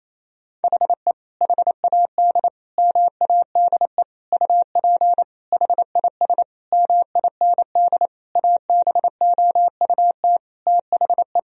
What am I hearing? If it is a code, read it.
Morse code, heard as a keyed tone, about 28 wpm, 700 Hz. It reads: HE HAD MADE UP HIS MIND ABOUT THE
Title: Sentences from Top 1000 Words - Review Part 3 - 28wpm